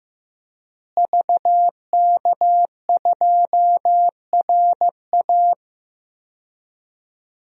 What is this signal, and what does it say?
Morse code, heard as a keyed tone, about 15 wpm, 700 Hz. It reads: VK2RA
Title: DXCC-100 Call Signs - 5 Characters - Repeated 3x - 15wpm